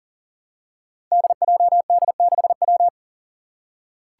Morse code, heard as a keyed tone, about 40 wpm, 700 Hz. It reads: DJD6W
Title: Five Challenging Letter-Number Combinations 40wpm